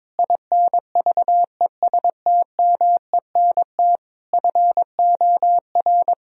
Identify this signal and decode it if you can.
Morse code, heard as a keyed tone, about 22 wpm, 700 Hz. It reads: INVESTMENT FOR
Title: Sentences from Top 1000 Words - Review Part 4 - 22wpm